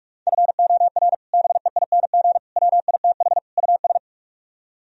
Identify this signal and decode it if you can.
Morse code, heard as a keyed tone, about 44 wpm, 700 Hz. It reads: FOR BEING WITH US